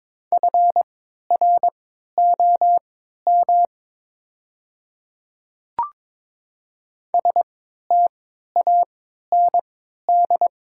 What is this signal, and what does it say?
Morse code, heard as a keyed tone, about 22 wpm, spaced as for 12 wpm, 700 Hz. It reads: FROM STAND
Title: 3x Character Spacing - Top 300 Words - 22wpm